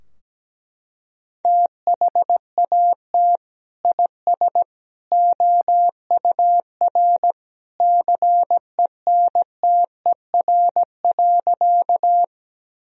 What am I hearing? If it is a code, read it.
Morse code, heard as a keyed tone, about 17 wpm, 700 Hz. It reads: THAT IS OUR CENTER.
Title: Sentences from Top 1000 Words - Review Part 2 - 17wpm